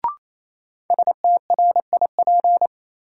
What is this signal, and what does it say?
Morse code, heard as a keyed tone, about 28 wpm, 700 Hz. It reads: STRIP
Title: Sentences from Top 1600 Words - New Word Drill Alt - 28wpm